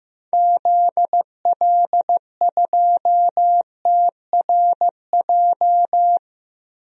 Morse code, heard as a keyed tone, about 15 wpm, 700 Hz. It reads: ZL2TRJ
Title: DXCC-100 Call Signs - 6 Characters - Repeated 2x - 15wpm